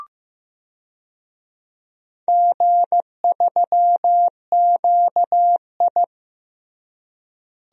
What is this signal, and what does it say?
Morse code, heard as a keyed tone, about 15 wpm, 700 Hz. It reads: G3QI
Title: DXCC-100 Call Signs - 4 Characters - Repeated 3x - 15wpm